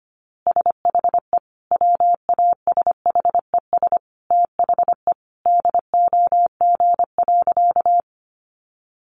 Morse code, heard as a keyed tone, about 25 wpm, 700 Hz. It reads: SHE WASHES THE DOG.
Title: Easy Phrases 25wpm